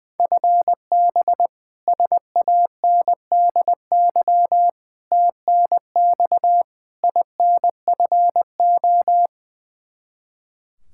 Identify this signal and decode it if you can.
Morse code, heard as a keyed tone, about 20 wpm, 700 Hz. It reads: FB SANDY TNX INFO
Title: Practice QSO - Exchange 3 - More Info - 20wpm